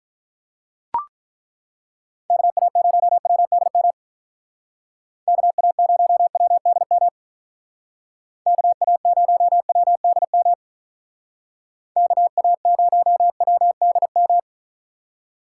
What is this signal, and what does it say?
Morse code, heard as a keyed tone, about 41 wpm, 700 Hz. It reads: KA0WDM KA0WDM KA0WDM KA0WDM
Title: Speed Racing - USA 2x3 Call Signs 35wpm